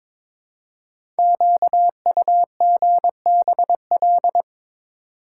Morse code, heard as a keyed tone, about 22 wpm, 700 Hz. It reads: QUGBL
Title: Code Groups - 5 Letters - 22wpm